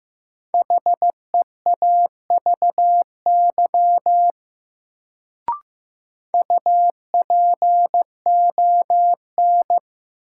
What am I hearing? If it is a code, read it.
Morse code, heard as a keyed tone, about 15 wpm, 700 Hz. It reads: HEAVY UPON
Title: Sentences from Top 1000 Words - Review Part 6 - New Word Drill - 15wpm